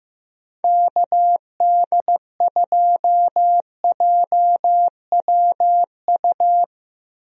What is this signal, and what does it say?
Morse code, heard as a keyed tone, about 15 wpm, 700 Hz. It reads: KD2JWU
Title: USA Call Signs - 6 Characters - Repeated 3x - 15wpm